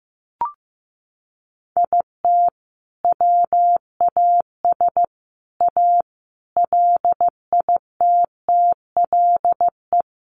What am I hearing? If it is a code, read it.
Morse code, heard as a keyed tone, about 15 wpm, 700 Hz. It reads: IT WAS A LITTLE
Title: Sentences from Top 1500 Words 15wpm